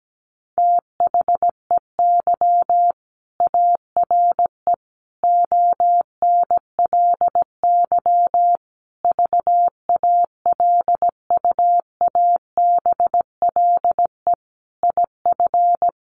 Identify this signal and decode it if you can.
Morse code, heard as a keyed tone, about 17 wpm, 700 Hz. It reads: THEY ARE ONLY VALUABLE IF
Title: Sentences from Top 1400 Words 17wpm